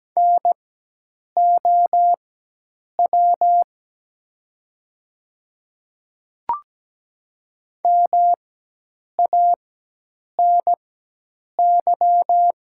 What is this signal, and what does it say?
Morse code, heard as a keyed tone, about 17 wpm, spaced as for 8 wpm, 700 Hz. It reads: NOW MANY
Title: 4x Character Spacing - Top 500 Words - 17wpm